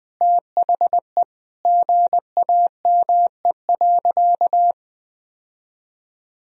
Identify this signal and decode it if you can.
Morse code, heard as a keyed tone, about 20 wpm, 700 Hz. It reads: THE GAME.
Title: Sentences from Top 1000 Words - Review Part 3 - 20wpm